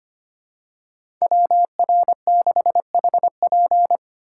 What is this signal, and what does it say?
Morse code, heard as a keyed tone, about 25 wpm, 700 Hz. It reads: WR6HP
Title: USA Call Signs - 8x Character Spacing - 25wpm